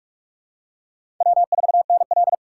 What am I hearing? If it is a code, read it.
Morse code, heard as a keyed tone, about 43 wpm, 700 Hz. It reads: W4NL
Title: USA Call Signs - 4 Characters - 6x Character Spacing - 45wpm